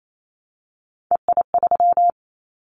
Morse code, heard as a keyed tone, about 28 wpm, 700 Hz. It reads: EI3
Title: Callsign Prefixes for Top 50 DXCC - 28wpm